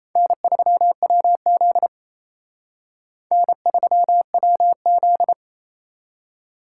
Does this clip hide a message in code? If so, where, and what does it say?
Yes: Morse code from 0.2 to 5.3 s, heard as a keyed tone, about 36 wpm, 700 Hz. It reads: N3WZ N3WZ